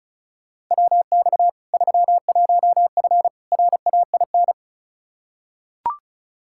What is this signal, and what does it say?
Morse code, heard as a keyed tone, about 36 wpm, 700 Hz. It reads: WX 31F RAIN
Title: Practice QSO - Exchange 2 - Info - 35wpm